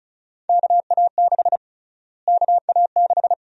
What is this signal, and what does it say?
Morse code, heard as a keyed tone, about 34 wpm, 700 Hz. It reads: KA6 KA6
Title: USA Call Sign Prefixes - Repeated 2x - 35wpm